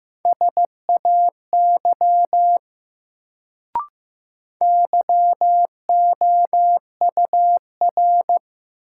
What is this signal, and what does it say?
Morse code, heard as a keyed tone, about 15 wpm, 700 Hz. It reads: SAY YOUR
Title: Sets of 2 Words - New Word Drill - 15wpm